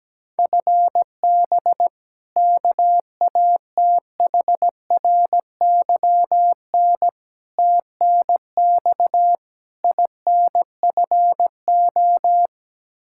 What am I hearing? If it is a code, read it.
Morse code, heard as a keyed tone, about 17 wpm, 700 Hz. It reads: FB KATHRYN TNX INFO
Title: Practice QSO - Exchange 3 - More Info - 17wpm